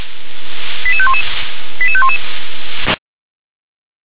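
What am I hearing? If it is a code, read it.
Five-tone selective call (ZVEI), heard as quick runs of five tones, tones of about 70 ms. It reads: 88511 88511
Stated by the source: ZVEI five-tone call received by SDR.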